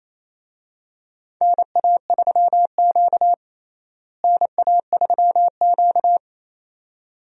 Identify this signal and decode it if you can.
Morse code, heard as a keyed tone, about 27 wpm, 700 Hz. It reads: NA3Q NA3Q
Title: USA Call Signs - 4 Characters - Repeated 2x - 28wpm